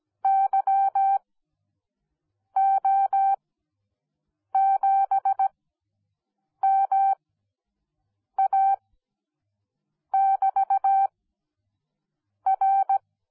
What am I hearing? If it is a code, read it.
Morse code, heard as a keyed tone, about 17 wpm, spaced as for 6 wpm, 780 Hz. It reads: YO7MA<BT>R
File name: Onlinelehrgang / morsen / DJ4UF Morsen10